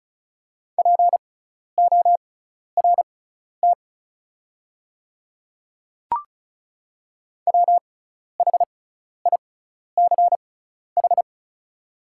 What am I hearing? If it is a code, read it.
Morse code, heard as a keyed tone, about 36 wpm, spaced as for 12 wpm, 690 Hz. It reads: PORT WHICH
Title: 6x Character Spacing - Top 200 Words - 35wpm